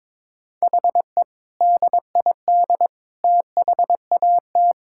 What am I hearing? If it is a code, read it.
Morse code, heard as a keyed tone, about 22 wpm, 700 Hz. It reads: HE DID THAT
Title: Sentences from Top 1000 Words - Review Part 5 - 22wpm